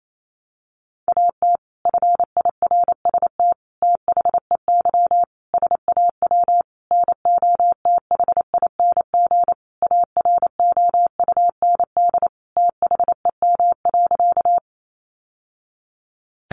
Morse code, heard as a keyed tone, about 28 wpm, 700 Hz. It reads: AT FIRST THEY SAW NOTHING AROUND THEM.
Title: Sentences from Top 600 Words 28wpm